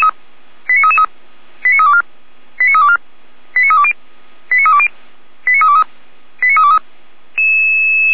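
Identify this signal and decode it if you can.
Five-tone selective call (ZVEI), heard as quick runs of five tones, tones of about 80 ms. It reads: 89333 89324 89324 89320 89320 89323 89323
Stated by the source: ZVEI five-tone selective call received on SDR.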